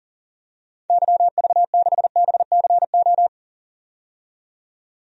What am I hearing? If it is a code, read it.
Morse code, heard as a keyed tone, about 40 wpm, 700 Hz. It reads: YV6BCO
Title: DXCC-100 Call Signs - 6 Characters - 40wpm